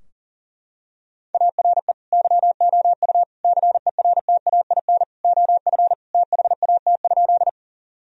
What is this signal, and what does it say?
Morse code, heard as a keyed tone, about 40 wpm, 700 Hz. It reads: ARE YOU CERTAIN OF THAT?
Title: Sentences from Top 1000 Words - Review Part 2 - 40wpm